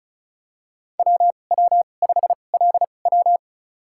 Morse code, heard as a keyed tone, about 34 wpm, spaced as for 25 wpm, 700 Hz. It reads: WW5LW
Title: USA Call Signs - 2x Character Spacing - 35wpm